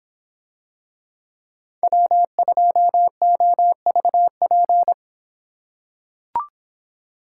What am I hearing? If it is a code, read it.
Morse code, heard as a keyed tone, about 24 wpm, 700 Hz. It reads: W2OVP